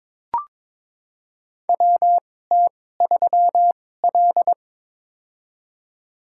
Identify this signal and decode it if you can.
Morse code, heard as a keyed tone, about 23 wpm, spaced as for 16 wpm, 700 Hz. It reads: WT3L
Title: USA Call Signs - 4 Characters - 2x Character Spacing - 22wpm